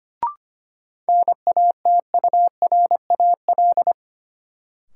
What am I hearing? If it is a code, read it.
Morse code, heard as a keyed tone, about 25 wpm, 700 Hz. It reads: NATURAL
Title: Sentences from Top 700 Words - New Word Drill - 25wpm